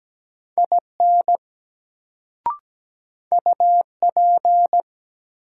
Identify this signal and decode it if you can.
Morse code, heard as a keyed tone, about 17 wpm, 700 Hz. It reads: IN UP
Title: Sentences from Top 100 Words - New Word Drill - 17wpm